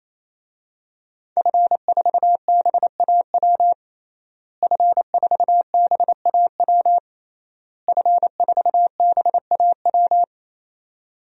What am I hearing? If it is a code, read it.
Morse code, heard as a keyed tone, about 28 wpm, 700 Hz. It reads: F4BAW F4BAW F4BAW
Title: DXCC-100 Call Signs - 5 Characters - Repeated 3x - 28wpm